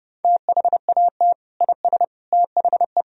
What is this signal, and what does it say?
Morse code, heard as a keyed tone, about 30 wpm, 700 Hz. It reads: THAT IS THE
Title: Sentences from Top 1000 Words - Review Part 2 - 30wpm